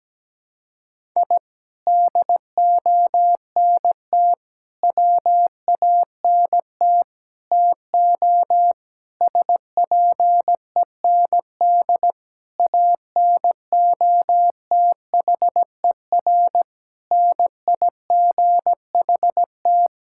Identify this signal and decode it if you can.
Morse code, heard as a keyed tone, about 17 wpm, 700 Hz. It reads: I DONT WANT TO SPEND ANOTHER NIGHT
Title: Sentences from Top 1000 Words - Review Part 5 - 17wpm